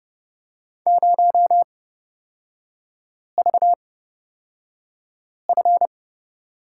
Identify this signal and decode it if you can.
Morse code, heard as a keyed tone, about 29 wpm, spaced as for 5 wpm, 700 Hz. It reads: 0VF
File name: Lesson 37e - Letter x - Three Characters 30wpm